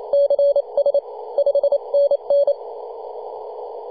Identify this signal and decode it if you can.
Morse code, heard as a keyed tone, about 28 wpm, 570 Hz. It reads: CS 5NN
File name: 2015/May/15 2222z PW0F 21.026MHz CW